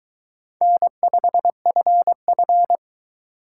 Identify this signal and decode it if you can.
Morse code, heard as a keyed tone, about 22 wpm, 700 Hz. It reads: N5FF